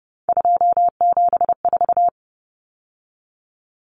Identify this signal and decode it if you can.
Morse code, heard as a keyed tone, about 30 wpm, 700 Hz. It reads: 274
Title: Numbers - 3-Digit 30wpm